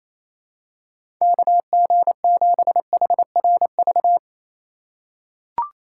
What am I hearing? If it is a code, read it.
Morse code, heard as a keyed tone, about 28 wpm, 700 Hz. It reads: KG7HRV
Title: USA Call Signs - 6 Characters - 4x Character Spacing - 28wpm